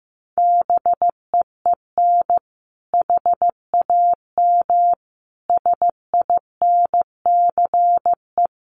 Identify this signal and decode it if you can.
Morse code, heard as a keyed tone, about 15 wpm, 700 Hz. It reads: BEEN HAM SINCE
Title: Common QSO Phrases 15wpm